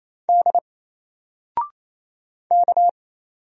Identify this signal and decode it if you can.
Morse code, heard as a keyed tone, about 27 wpm, 700 Hz. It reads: D K
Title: Single Letter-Number 28wpm